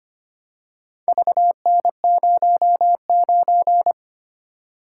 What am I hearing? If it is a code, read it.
Morse code, heard as a keyed tone, about 25 wpm, 700 Hz. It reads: VN09
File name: practice / 25wpm / Four Letter-Number Combinations - 25wpm